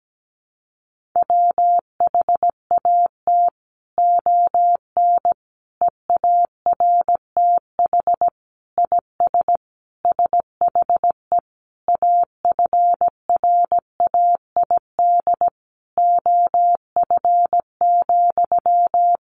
Morse code, heard as a keyed tone, about 17 wpm, 700 Hz. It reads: WHAT ON EARTH IS SHE AFRAID OF,
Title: Sentences from Top 1500 Words 17wpm